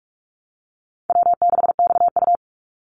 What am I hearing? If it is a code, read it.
Morse code, heard as a keyed tone, about 44 wpm, 700 Hz. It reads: W6XU
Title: Easy Call Signs - 45wpm